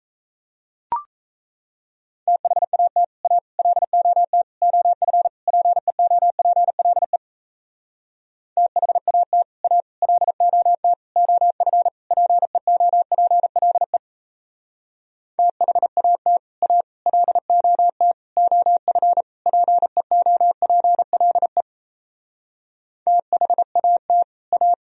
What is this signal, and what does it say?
Morse code, heard as a keyed tone, about 38 wpm, 700 Hz. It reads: THAT A LOT OF PEOPLE THAT A LOT OF PEOPLE THAT A LOT OF PEOPLE THAT A